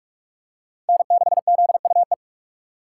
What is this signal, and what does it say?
Morse code, heard as a keyed tone, about 44 wpm, 700 Hz. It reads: N6ZUE